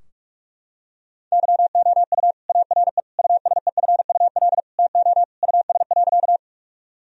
Morse code, heard as a keyed tone, about 41 wpm, 700 Hz. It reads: YOU ARE USEFUL TO US.
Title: Sentences from Top 1000 Words - Review Part 4 - 45wpm